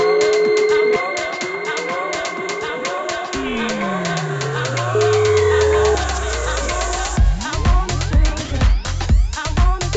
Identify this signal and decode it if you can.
Ringback tone (Europe, 425 Hz), with noise.